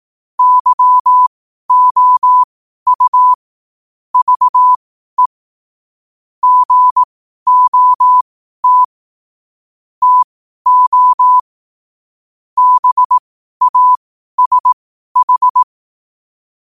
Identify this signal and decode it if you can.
Morse code, heard as a keyed tone, about 18 wpm, spaced as for 12 wpm, 1000 Hz. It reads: YOU VE GOT TO BASH
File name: 2026-01-07-12wpm